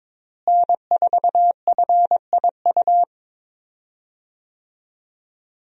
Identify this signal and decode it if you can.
Morse code, heard as a keyed tone, about 22 wpm, 700 Hz. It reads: N4FIU
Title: USA Call Signs - 5 Characters - Repeated 2x - 22wpm